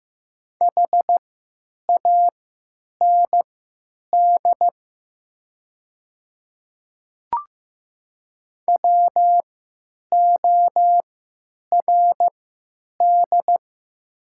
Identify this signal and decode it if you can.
Morse code, heard as a keyed tone, about 15 wpm, spaced as for 8 wpm, 700 Hz. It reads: HAND WORD
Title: 3x Character Spacing - Top 200 Words - 15wpm